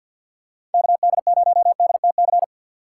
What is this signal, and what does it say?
Morse code, heard as a keyed tone, about 50 wpm, 700 Hz. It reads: KD0DTC
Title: USA Call Signs - 6 Characters - 3x Character Spacing - 50wpm